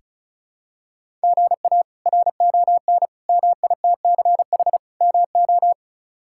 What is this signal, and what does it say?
Morse code, heard as a keyed tone, about 36 wpm, 700 Hz. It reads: GA RON MITCH MO
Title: K1USN SST QSO Practice - 35wpm